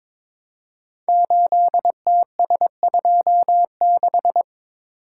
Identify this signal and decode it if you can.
Morse code, heard as a keyed tone, about 22 wpm, 700 Hz. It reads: 8TS26
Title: Code Groups - 5 Letter:Numbers - 22wpm